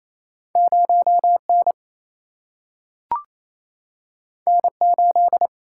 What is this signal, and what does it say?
Morse code, heard as a keyed tone, about 28 wpm, 700 Hz. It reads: 0N N8
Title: Single-Letter Single-Number 28wpm